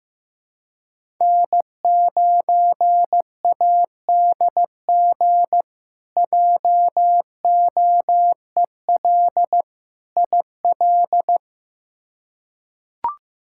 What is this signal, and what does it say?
Morse code, heard as a keyed tone, about 15 wpm, 700 Hz. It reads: N9ADG JOEL IL
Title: K1USN SST QSO Practice - 15wpm